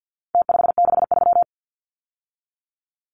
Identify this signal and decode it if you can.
Morse code, heard as a keyed tone, about 49 wpm, 700 Hz. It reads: T563
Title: Numbers - Contest Style - 4-Digit 50wpm